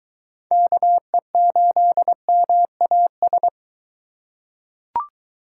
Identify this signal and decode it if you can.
Morse code, heard as a keyed tone, about 21 wpm, 700 Hz. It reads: KE8MAS